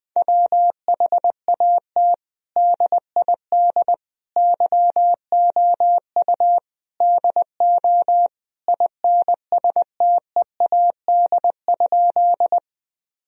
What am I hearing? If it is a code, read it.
Morse code, heard as a keyed tone, about 20 wpm, 700 Hz. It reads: WHAT DID YOU DO INSTEAD?
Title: Sentences from Top 1000 Words - Review Part 4 - 20wpm